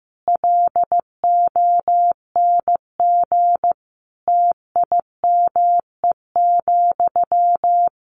Morse code, heard as a keyed tone, about 15 wpm, 700 Hz. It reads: LONG TIME,
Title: A Sound Of Thunder 15wpm